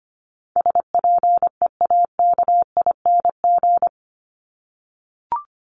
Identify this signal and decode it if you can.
Morse code, heard as a keyed tone, about 25 wpm, 700 Hz. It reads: SPEAKING
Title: ING Suffix - Top 100 Words with Base Words 25wpm